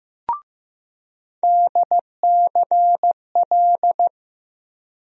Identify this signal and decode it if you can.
Morse code, heard as a keyed tone, about 15 wpm, 700 Hz. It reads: DCL
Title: USA Call Sign Postfixes - 3 Letters 15wpm